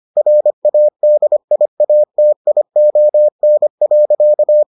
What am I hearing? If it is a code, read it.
Morse code, heard as a keyed tone, about 25 wpm, 600 Hz. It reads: RADIATION.